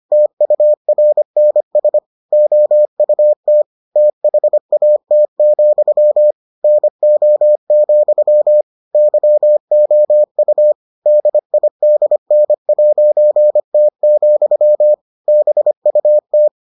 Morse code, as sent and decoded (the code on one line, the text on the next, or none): - ..- .-. -. ... / --- ..- - / - .... .- - --..-- / -. --- --..-- / -.-- --- ..- / -.. .. -.. -. .----. - --..-- / -... ..- -
TURNS OUT THAT, NO, YOU DIDN'T, BUT